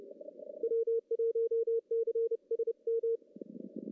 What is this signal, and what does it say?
Morse code, heard as a keyed tone, about 30 wpm, 460 Hz. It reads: W1CSM
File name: CW / cqww-cw-k5tr